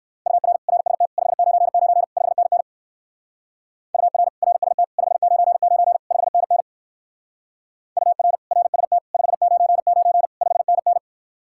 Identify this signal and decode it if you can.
Morse code, heard as a keyed tone, about 59 wpm, 700 Hz. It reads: UR RST 599 5NN UR RST 599 5NN UR RST 599 5NN